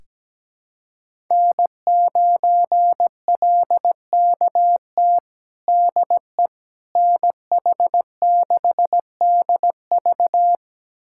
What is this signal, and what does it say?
Morse code, heard as a keyed tone, about 17 wpm, 700 Hz. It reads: N9LKT DE NH6DV
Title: USA Callsign DE Callsign - 5 Characters - 17wpm